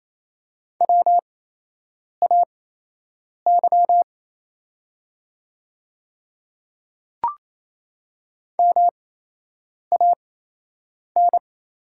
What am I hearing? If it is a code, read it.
Morse code, heard as a keyed tone, about 27 wpm, spaced as for 8 wpm, 700 Hz. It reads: WAY MAN
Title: 8x Character Spacing - Top 200 Words - 28wpm